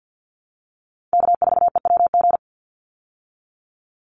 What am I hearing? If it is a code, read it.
Morse code, heard as a keyed tone, about 50 wpm, 700 Hz. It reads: K4EWG
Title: Call Signs Repeated 2x - 50wpm